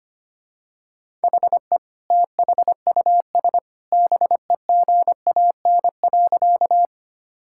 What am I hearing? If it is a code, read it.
Morse code, heard as a keyed tone, about 25 wpm, 700 Hz. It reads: HE THUS BEGAN.
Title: Sentences from Top 1000 Words - Review Part 5 - 25wpm